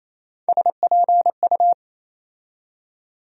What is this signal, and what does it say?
Morse code, heard as a keyed tone, about 28 wpm, 700 Hz. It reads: SPU
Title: USA Call Sign Postfixes - 3 Letters 28wpm